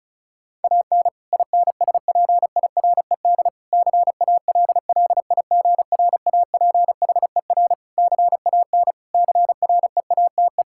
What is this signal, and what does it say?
Morse code, heard as a keyed tone, about 34 wpm, 700 Hz. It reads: AN INSPIRED CALLIGRAPHER CAN CREATE
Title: Pangram Challenge v2 - 35wpm